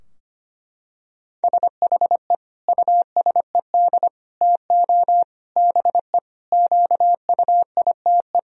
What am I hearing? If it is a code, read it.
Morse code, heard as a keyed tone, about 25 wpm, 700 Hz. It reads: SHE USED TO BE QUITE
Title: Sentences from Top 1000 Words - Review Part 2 - 25wpm